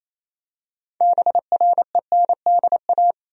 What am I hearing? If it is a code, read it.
Morse code, heard as a keyed tone, about 27 wpm, 700 Hz. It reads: BRENDA
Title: USA Common Names - 28wpm